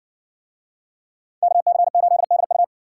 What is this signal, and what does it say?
Morse code, heard as a keyed tone, about 60 wpm, 700 Hz. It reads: KC8DU